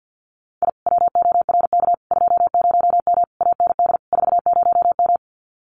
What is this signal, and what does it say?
Morse code, heard as a keyed tone, about 50 wpm, 700 Hz. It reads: I WORK 20M AND 40M
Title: Common QSO Phrases Encore 50wpm